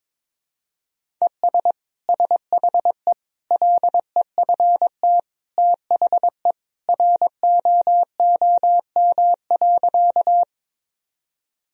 Morse code, heard as a keyed tone, about 22 wpm, 700 Hz. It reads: ES SHE LEFT THE ROOM.
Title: Sentences from Top 300 Words and Abbreviations 22wpm